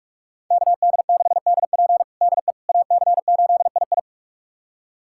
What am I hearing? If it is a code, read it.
Morse code, heard as a keyed tone, about 46 wpm, 700 Hz. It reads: KD6DP DE AC8II
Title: USA Callsign DE Callsign - 5 Characters - 45wpm